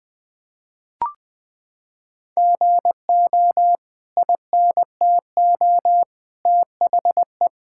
Morse code, heard as a keyed tone, about 21 wpm, 700 Hz. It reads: GO INTO THE
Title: Sentences from Top 1000 Words - Review Part 5 - 20wpm